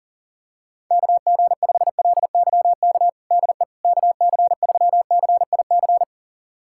Morse code, heard as a keyed tone, about 40 wpm, 700 Hz. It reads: KG5LYK DE KC3CIC
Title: USA Callsign DE Callsign - 6 Characters - 40wpm